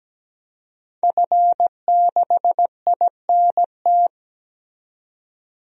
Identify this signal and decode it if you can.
Morse code, heard as a keyed tone, about 17 wpm, 700 Hz. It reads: F6INT
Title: DXCC-100 Call Signs - 5 Characters - 17wpm